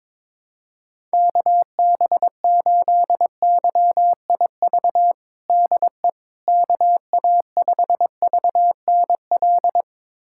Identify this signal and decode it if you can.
Morse code, heard as a keyed tone, about 22 wpm, 700 Hz. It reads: KB8YIV DE KA5VNL
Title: USA Callsign DE Callsign - 6 Characters - 22wpm